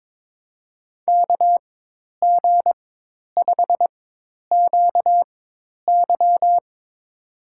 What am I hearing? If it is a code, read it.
Morse code, heard as a keyed tone, about 22 wpm, spaced as for 10 wpm, 700 Hz. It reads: KG5QY
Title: USA Call Signs - 5 Characters - 4x Character Spacing - 22wpm